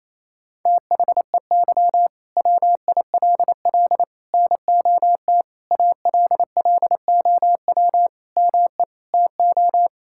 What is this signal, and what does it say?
Morse code, heard as a keyed tone, about 27 wpm, 700 Hz. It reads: THEY WILL NOT ALLOW ME TO
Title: Sentences from Top 1000 Words - Review Part 5 - 28wpm